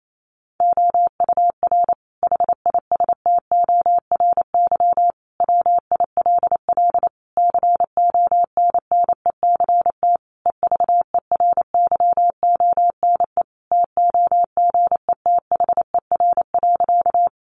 Morse code, heard as a keyed tone, about 28 wpm, 700 Hz. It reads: OUR HISTORY WILL CONNECT EVERYONE TOGETHER.
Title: Sentences from Top 1500 Words 28wpm